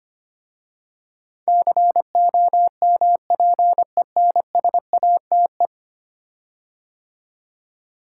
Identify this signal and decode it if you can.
Morse code, heard as a keyed tone, about 25 wpm, 700 Hz. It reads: COMPENSATE
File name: 10 Letter Words - First 3 Letters Given - 25wpm